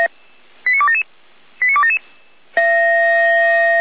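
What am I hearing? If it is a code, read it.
Five-tone selective call (ZVEI), heard as quick runs of five tones, tones of about 80 ms. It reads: 89277 89277
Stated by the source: ZVEI five-tone selective call, received on SDR.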